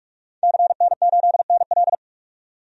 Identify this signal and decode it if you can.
Morse code, heard as a keyed tone, about 45 wpm, 700 Hz. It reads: CN8NL
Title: DXCC-100 Call Signs - 5 Characters - 45wpm